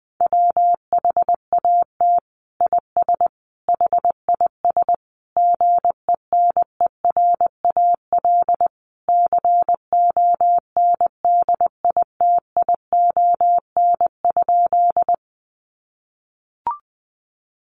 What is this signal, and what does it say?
Morse code, heard as a keyed tone, about 20 wpm, 700 Hz. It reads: WHAT IS HIS GENERAL CONDITION?
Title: Sentences from Top 1000 Words 20wpm